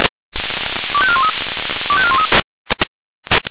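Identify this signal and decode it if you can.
Five-tone selective call (ZVEI), heard as quick runs of five tones, tones of about 70 ms. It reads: 25413 25413
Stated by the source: ZVEI five-tone call 25304 received on SDR.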